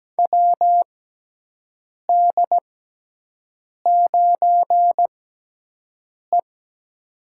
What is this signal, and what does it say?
Morse code, heard as a keyed tone, about 17 wpm, spaced as for 6 wpm, 700 Hz. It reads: WD9E
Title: USA Call Signs - 4 Characters - 6x Character Spacing - 17wpm